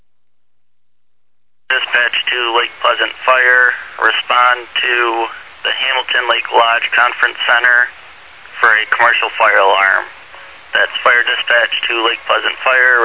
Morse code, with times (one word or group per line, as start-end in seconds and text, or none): none